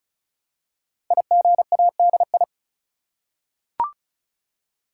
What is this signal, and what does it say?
Morse code, heard as a keyed tone, about 36 wpm, 700 Hz. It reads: IGADI
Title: Code Groups - 5 Letters - 35wpm